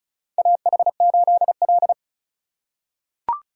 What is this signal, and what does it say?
Morse code, heard as a keyed tone, about 36 wpm, 700 Hz. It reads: AH8L
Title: USA Call Signs - 4 Characters - 2x Character Spacing - 35wpm